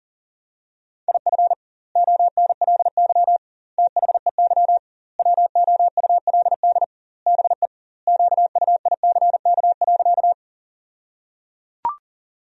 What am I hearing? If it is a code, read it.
Morse code, heard as a keyed tone, about 38 wpm, 700 Hz. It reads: IF ONLY THEY WOULD BE QUICK.
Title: 2x Word-Spacing - Sentences from Top 500 Words - 40wpm